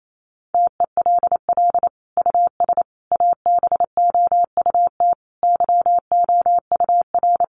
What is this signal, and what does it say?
Morse code, heard as a keyed tone, about 28 wpm, 700 Hz. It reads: TELL US ABOUT YOUR
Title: Sentences from Top 1400 Words 28wpm